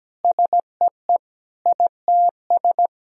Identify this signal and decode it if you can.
Morse code, heard as a keyed tone, about 17 wpm, 700 Hz. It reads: SEE ITS
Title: Sentences from Top 1000 Words - Review Part 3 - 17wpm